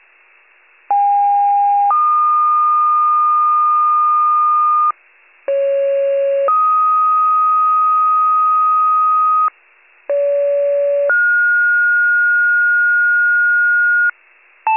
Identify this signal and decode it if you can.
Two-tone page, heard as a one-second tone, then a three-second tone, about 810 then 1220 Hz, about 550 then 1250 Hz, about 570 then 1430 Hz. It is heard from 0.9 to 14.1 s.
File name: wcalert